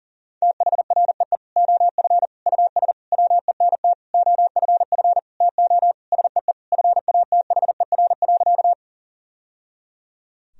Morse code, heard as a keyed tone, about 37 wpm, 700 Hz. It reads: THREE OF US WENT OFF TO SEE FATHER.
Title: Sentences from Top 1000 Words - Review Part 1 - 40wpm